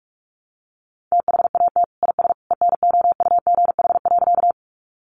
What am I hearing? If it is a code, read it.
Morse code, heard as a keyed tone, about 44 wpm, 700 Hz. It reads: THAT IS ENOUGH.
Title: Sentences from Top 400 Words 45wpm